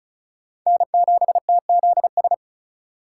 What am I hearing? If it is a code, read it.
Morse code, heard as a keyed tone, about 36 wpm, 700 Hz. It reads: N7TZS
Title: USA Call Signs - 5 Characters - 2x Character Spacing - 35wpm